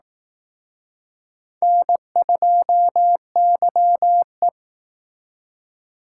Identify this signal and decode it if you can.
Morse code, heard as a keyed tone, about 19 wpm, 700 Hz. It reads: N2YE
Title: Speed Racing - USA Call Signs 15wpm